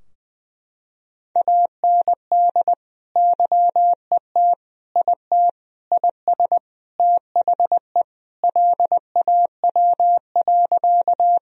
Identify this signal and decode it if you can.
Morse code, heard as a keyed tone, about 20 wpm, 700 Hz. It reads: AND YET IT IS THE LAW.
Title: Sentences from Top 1000 Words - Review Part 3 - 20wpm